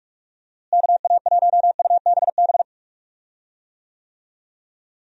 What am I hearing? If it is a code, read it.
Morse code, heard as a keyed tone, about 42 wpm, 700 Hz. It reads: KA1UBB
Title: USA Call Signs - 6 Characters - Repeated 2x - 45wpm